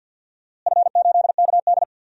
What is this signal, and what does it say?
Morse code, heard as a keyed tone, about 52 wpm, 700 Hz. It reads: F8KD
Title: DXCC-100 Call Signs - 4 Characters - Repeated 2x - 50wpm